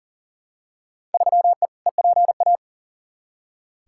Morse code, heard as a keyed tone, about 40 wpm, 700 Hz. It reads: 3E EPA
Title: Field Day Exchanges Encore - 40wpm